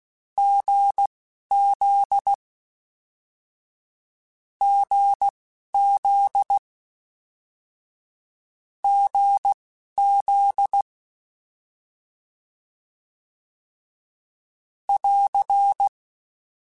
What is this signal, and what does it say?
Morse code, heard as a keyed tone, about 16 wpm, spaced as for 11 wpm, 800 Hz. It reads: GZ GZ GZ <AR>
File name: Short Runs / Short Runs-mp3s / GZ GZ7 @10